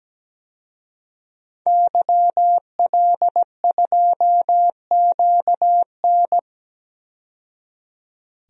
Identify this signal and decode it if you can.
Morse code, heard as a keyed tone, about 17 wpm, 700 Hz. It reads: YL2QN
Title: DXCC-100 Call Signs - 5 Characters - 17wpm